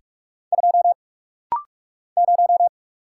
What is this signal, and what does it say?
Morse code, heard as a keyed tone, about 44 wpm, 700 Hz. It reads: 2 0
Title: Single Letter-Number Alt - Rapid-Fire 45wpm